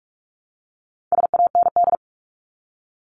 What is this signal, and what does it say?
Morse code, heard as a keyed tone, about 44 wpm, 700 Hz. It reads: SAND
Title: Top 1501-1750 Words 45wpm